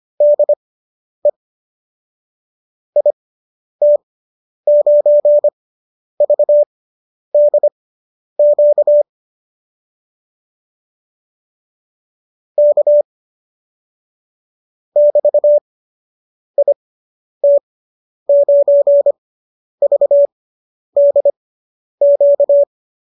Morse code, heard as a keyed tone, about 25 wpm, spaced as for 10 wpm, 590 Hz. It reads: DE IT9VDQ K <BT> IT9VDQ